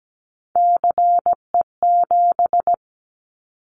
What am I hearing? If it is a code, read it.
Morse code, heard as a keyed tone, about 17 wpm, 700 Hz. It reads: CE7
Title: Callsign Prefixes for Top 100 DXCC - 17wpm